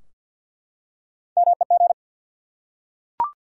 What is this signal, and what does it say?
Morse code, heard as a keyed tone, about 51 wpm, 700 Hz. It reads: MEG
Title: USA Common Names - Max 3 Letters - 50wpm